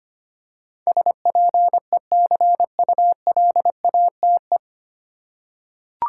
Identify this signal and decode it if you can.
Morse code, heard as a keyed tone, about 25 wpm, 700 Hz. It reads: SPECULATE
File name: Nine Letter Words - 25wpm